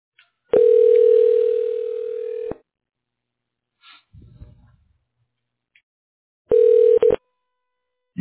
Ringback tone (North America), 440 + 480 Hz.